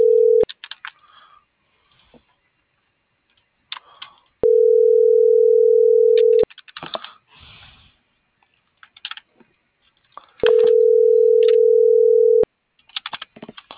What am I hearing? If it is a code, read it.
Ringback tone (North America), 440 + 480 Hz.